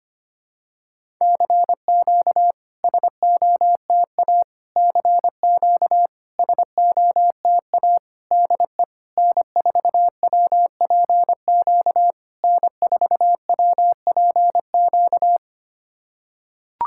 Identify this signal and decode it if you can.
Morse code, heard as a keyed tone, about 25 wpm, 690 Hz. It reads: CQ SOTA CQ SOTA DE N4WPQ N4WPQ
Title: SOTA QSO Practice 25wpm